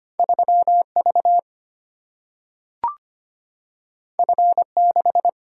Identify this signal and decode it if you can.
Morse code, heard as a keyed tone, about 25 wpm, 700 Hz. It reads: 3 V F 6
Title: Single-Letter Single-Number 25wpm